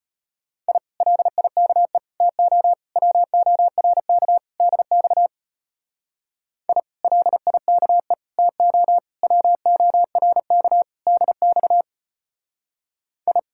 Morse code, heard as a keyed tone, about 31 wpm, 700 Hz. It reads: I LIKE TO WORK DX I LIKE TO WORK DX I
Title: Speed-Racing Common QSO Phrases - 25wpm